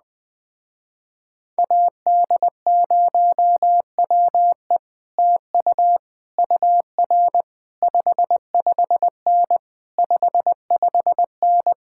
Morse code, heard as a keyed tone, about 20 wpm, 700 Hz. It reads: AD0WE TU UR 55N 55N
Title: Speed-Racing Common QSO Phrases Encore - 17wpm